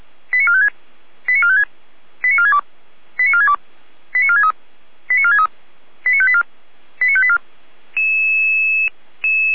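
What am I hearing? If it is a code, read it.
Five-tone selective call (ZVEI), heard as quick runs of five tones, tones of about 70 ms. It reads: 89456 89456 89462 89462 89463 89463 89584 89584
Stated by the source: Five-tone ZVEI selective call received on SDR.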